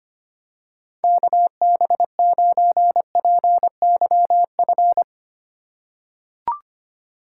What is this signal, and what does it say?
Morse code, heard as a keyed tone, about 25 wpm, 700 Hz. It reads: KB9PYF
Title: USA Call Signs - 6 Characters - 4x Character Spacing - 25wpm